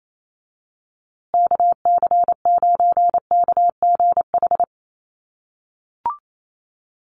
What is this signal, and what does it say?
Morse code, heard as a keyed tone, about 28 wpm, 700 Hz. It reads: KC9KGH
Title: USA Call Signs Repeated 3x 28wpm